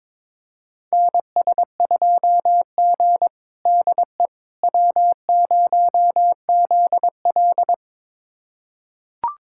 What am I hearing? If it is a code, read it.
Morse code, heard as a keyed tone, about 22 wpm, 700 Hz. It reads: NS2G DE W0ZL
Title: USA Callsign DE Callsign - 4 Characters - 22wpm